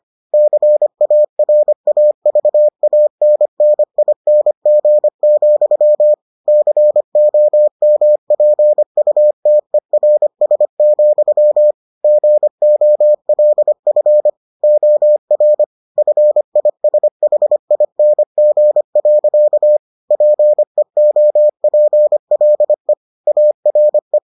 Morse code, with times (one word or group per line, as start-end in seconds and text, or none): 0.3-6.1 CARAVANNING,
6.5-11.7 COMPUTERS,
12.0-14.3 GOLF
14.6-15.6 OR
16.0-19.8 FISHING.
20.1-22.9 PEOPLE
23.3-24.2 ARE